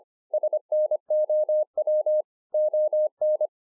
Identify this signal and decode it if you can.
Morse code, heard as a keyed tone, about 25 wpm, 610 Hz. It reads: SNOW ON